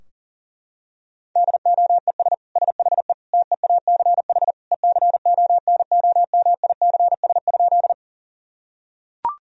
Morse code, heard as a keyed tone, about 40 wpm, 700 Hz. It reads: DOES SHE TEACH ECONOMICS?
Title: Sentences from Top 1000 Words - Review Part 3 - 40wpm